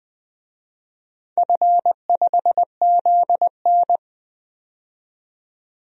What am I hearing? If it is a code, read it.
Morse code, heard as a keyed tone, about 20 wpm, 700 Hz. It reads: F5ZN
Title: DXCC-100 Call Signs - 4 Characters - 20wpm